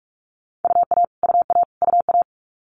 Morse code, heard as a keyed tone, about 46 wpm, 700 Hz. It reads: UA UA UA
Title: Multi-letter Phonemes Repeated 3x 45wpm